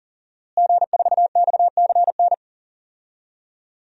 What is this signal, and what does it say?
Morse code, heard as a keyed tone, about 40 wpm, 690 Hz. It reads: G4XCN
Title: DXCC-100 Call Signs - 5 Characters - Repeated 2x - 40wpm